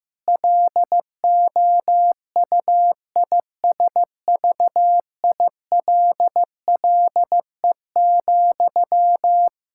Morse code, heard as a keyed tone, about 15 wpm, 700 Hz. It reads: LOUISVILLE,
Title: Practice QSO - Exchange 1 - Beginning - 15wpm